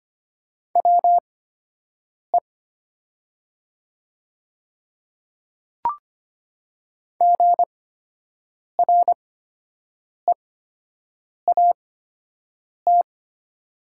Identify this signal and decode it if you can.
Morse code, heard as a keyed tone, about 25 wpm, spaced as for 7 wpm, 700 Hz. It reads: WE GREAT
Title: 8x Character Spacing - Top 200 Words - 25wpm